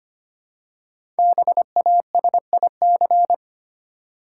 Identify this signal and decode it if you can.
Morse code, heard as a keyed tone, about 25 wpm, 700 Hz. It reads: BASIC
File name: 5 Letter Words - First 3 Letters Given - 25wpm